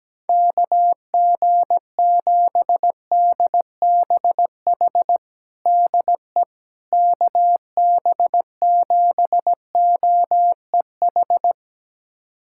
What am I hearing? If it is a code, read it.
Morse code, heard as a keyed tone, about 17 wpm, 700 Hz. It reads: KG7DBH DE KB7OEH
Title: USA Callsign DE Callsign - 6 Characters - 17wpm